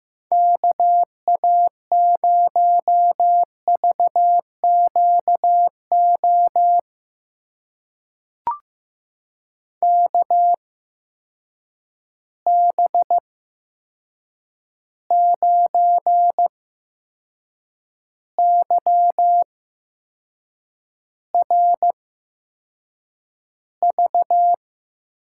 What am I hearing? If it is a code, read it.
Morse code, heard as a keyed tone, about 15 wpm, spaced as for 6 wpm, 700 Hz. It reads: KA0VQO KB9YRV
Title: USA Call Signs - 6 Characters - 8x Character Spacing - 15wpm